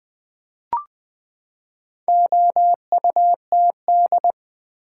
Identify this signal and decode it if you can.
Morse code, heard as a keyed tone, about 20 wpm, 700 Hz. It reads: OUTD
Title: Build A Word - 4 Letters At A Time - 8-Letter Words - 20wpm